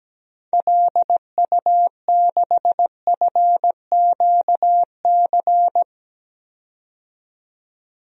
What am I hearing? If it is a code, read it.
Morse code, heard as a keyed tone, about 17 wpm, 700 Hz. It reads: LU6FQC
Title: DXCC-100 Call Signs - 6 Characters - 17wpm